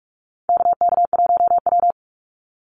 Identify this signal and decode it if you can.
Morse code, heard as a keyed tone, about 44 wpm, 700 Hz. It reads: KK1W
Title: NA SOTA Activator Callsigns 45wpm